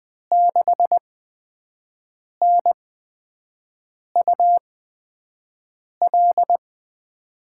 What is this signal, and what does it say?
Morse code, heard as a keyed tone, about 20 wpm, spaced as for 5 wpm, 700 Hz. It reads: 6NUL
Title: USA Call Signs - 6 Characters - 8x Character Spacing - 20wpm